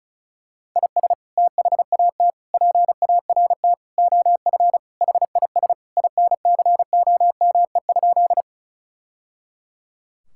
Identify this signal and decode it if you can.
Morse code, heard as a keyed tone, about 36 wpm, 700 Hz. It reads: IS THAT PART OF HIS INCOME?
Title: Sentences from Top 1000 Words - Review Part 5 - 35wpm